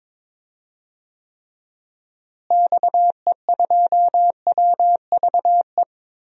Morse code, heard as a keyed tone, about 22 wpm, 700 Hz. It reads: XE2WVE
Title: DXCC-100 Call Signs - 6 Characters - Repeated 2x - 22wpm